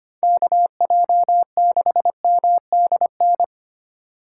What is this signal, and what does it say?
Morse code, heard as a keyed tone, about 33 wpm, 700 Hz. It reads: KJ6MDN